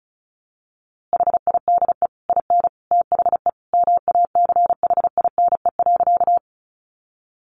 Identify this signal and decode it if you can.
Morse code, heard as a keyed tone, about 34 wpm, 700 Hz. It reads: HIDE IN THE MACHINE.
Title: A Sound Of Thunder 35wpm